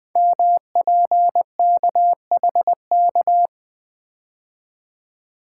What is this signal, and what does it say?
Morse code, heard as a keyed tone, about 20 wpm, 700 Hz. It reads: MPKHK
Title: Code Groups - 5 Letter:Numbers - 20wpm